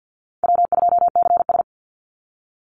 Morse code, heard as a keyed tone, about 48 wpm, 700 Hz. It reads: W2CS
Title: Call Signs - 50wpm